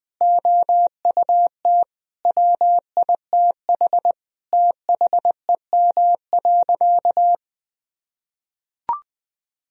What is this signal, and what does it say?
Morse code, heard as a keyed tone, about 21 wpm, 700 Hz. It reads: OUT WITH THEM.
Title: Sentences from Top 1000 Words - Review Part 3 - 20wpm